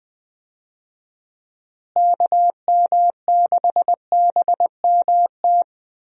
Morse code, heard as a keyed tone, about 21 wpm, 700 Hz. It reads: KM6BMT
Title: USA Call Signs - 6 Characters - Repeated 3x - 20wpm